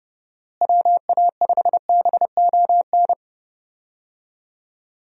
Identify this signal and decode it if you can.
Morse code, heard as a keyed tone, about 31 wpm, 700 Hz. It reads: WA5BON
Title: USA Call Signs - 6 Characters - Repeated 2x - 30wpm